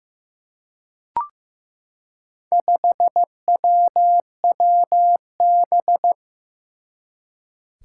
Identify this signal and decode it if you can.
Morse code, heard as a keyed tone, about 15 wpm, 700 Hz. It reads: 5WWB
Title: Four Challenging Letter-Number Combinations 15wpm